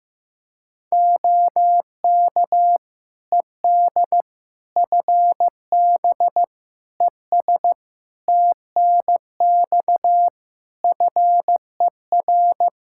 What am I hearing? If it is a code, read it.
Morse code, heard as a keyed tone, about 15 wpm, 700 Hz. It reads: OK ED FB ES TNX FER
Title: K1USN SST QSO Practice 15wpm